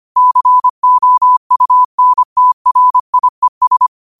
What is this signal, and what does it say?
Morse code, heard as a keyed tone, about 25 wpm, 1000 Hz. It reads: COUNTRIES